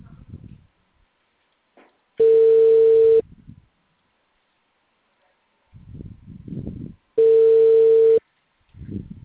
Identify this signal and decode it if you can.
Ringback tone (China, 450 Hz).
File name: out- -2004- - - .98728